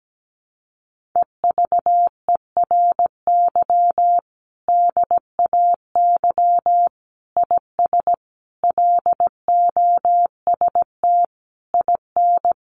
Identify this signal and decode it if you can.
Morse code, heard as a keyed tone, about 17 wpm, 700 Hz. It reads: EVERY DAY IS LOST IN
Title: Sentences from Top 1400 Words 17wpm